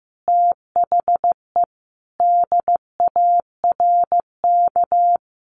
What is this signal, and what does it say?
Morse code, heard as a keyed tone, about 15 wpm, 700 Hz. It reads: THE DARK
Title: Sentences from Top 600 Words 15wpm